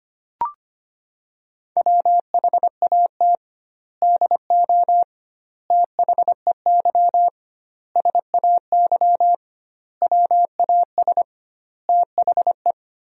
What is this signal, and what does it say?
Morse code, heard as a keyed tone, about 25 wpm, 700 Hz. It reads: WHAT DO THEY SAY WAS THE
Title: 2x Word-Spacing - Sentences from Top 200 Words - 25wpm